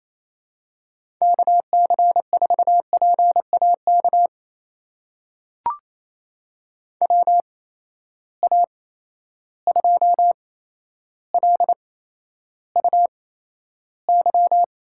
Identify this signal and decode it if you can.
Morse code, heard as a keyed tone, about 28 wpm, spaced as for 12 wpm, 700 Hz. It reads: KC4PAK WA2LUY
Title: USA Call Signs - 6 Characters - 8x Character Spacing - 28wpm